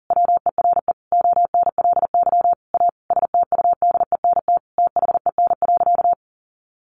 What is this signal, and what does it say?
Morse code, heard as a keyed tone, about 40 wpm, 700 Hz. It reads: WERE ONLY A STUDENT THEN.
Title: Sentences from Top 1000 Words 40wpm